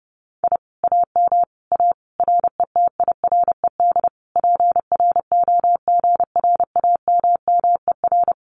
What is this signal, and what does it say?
Morse code, heard as a keyed tone, about 30 wpm, 700 Hz. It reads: I AM A RETIRED PROGRAMMER
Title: Common QSO Phrases Encore 30wpm